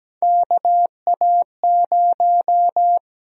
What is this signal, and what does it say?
Morse code, heard as a keyed tone, about 17 wpm, 700 Hz. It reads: KA0
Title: USA Call Sign Prefixes - Repeated 2x - 17wpm